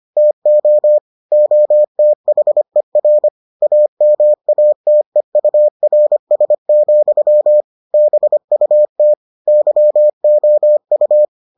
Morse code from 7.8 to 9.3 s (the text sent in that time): BUT